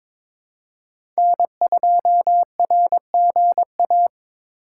Morse code, heard as a keyed tone, about 23 wpm, 700 Hz. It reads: N2RGA
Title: USA Call Signs - 5 Characters - 4x Character Spacing - 22wpm